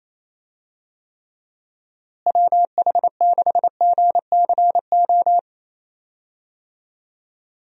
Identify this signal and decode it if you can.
Morse code, heard as a keyed tone, about 27 wpm, 700 Hz. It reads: WH6GCO
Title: USA Call Signs - 6 Characters - Repeated 3x - 28wpm